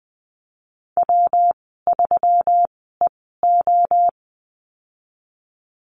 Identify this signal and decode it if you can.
Morse code, heard as a keyed tone, about 20 wpm, spaced as for 15 wpm, 700 Hz. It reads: W3EO
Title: 122 Lesson 20g - Number 3 - Call Signs 20wpm